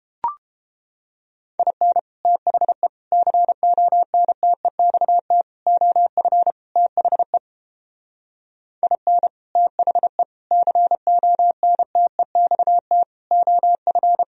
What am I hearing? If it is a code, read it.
Morse code, heard as a keyed tone, about 31 wpm, 700 Hz. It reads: IN THE CONTEXT OF THE IN THE CONTEXT OF